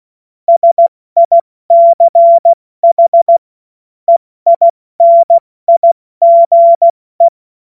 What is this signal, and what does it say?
Morse code, heard as a keyed tone, about 16 wpm, 680 Hz. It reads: SICH EINIGE
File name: Morsekurs / Witze / Witz 12 März 2024 70 F80